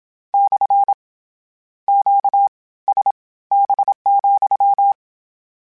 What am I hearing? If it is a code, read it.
Morse code, heard as a keyed tone, about 27 wpm, spaced as for 18 wpm, 800 Hz. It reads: / QSB,
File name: Q Calls / Q Calls-mp3s / Q CALLS SPSL @25